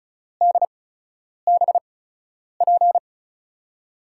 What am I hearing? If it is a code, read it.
Morse code, heard as a keyed tone, about 36 wpm, spaced as for 10 wpm, 700 Hz. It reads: DBP
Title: USA Call Signs - 6 Characters - 8x Character Spacing - 35wpm